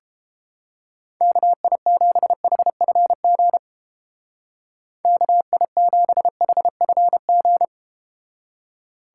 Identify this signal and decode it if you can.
Morse code, heard as a keyed tone, about 31 wpm, 700 Hz. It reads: KI7HFG KI7HFG